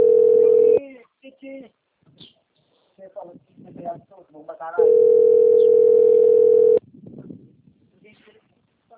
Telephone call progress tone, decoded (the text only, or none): ringback tone